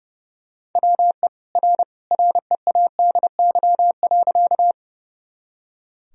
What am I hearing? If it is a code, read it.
Morse code, heard as a keyed tone, about 30 wpm, 700 Hz. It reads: WE R READY.
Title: Sentences from Top 400 Words and Abbreviations 30wpm